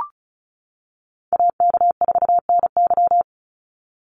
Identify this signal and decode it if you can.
Morse code, heard as a keyed tone, about 34 wpm, 700 Hz. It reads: AK4NY
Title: Call Signs - 35wpm